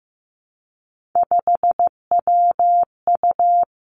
Morse code, heard as a keyed tone, about 15 wpm, 700 Hz. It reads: 5WU
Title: Three Challenging Letter-Number Combinations 15wpm